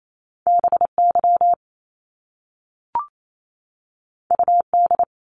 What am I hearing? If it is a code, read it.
Morse code, heard as a keyed tone, about 28 wpm, 700 Hz. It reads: B Y U D
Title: Top Two Letter Combinations in Words 28wpm